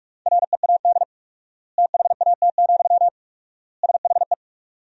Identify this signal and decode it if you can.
Morse code, heard as a keyed tone, about 46 wpm, 700 Hz. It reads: READ THAT, SHE
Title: 4x Word-Spacing - Sentences from Top 200 Words - 45wpm